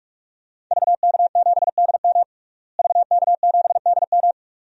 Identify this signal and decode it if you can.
Morse code, heard as a keyed tone, about 46 wpm, 700 Hz. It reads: VK7DM VK7DM
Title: DXCC-100 Call Signs - 5 Characters - Repeated 2x - 45wpm